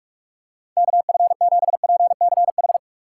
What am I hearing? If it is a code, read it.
Morse code, heard as a keyed tone, about 46 wpm, 700 Hz. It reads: KF7PCH
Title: USA Call Signs - 4x Character Spacing - 45wpm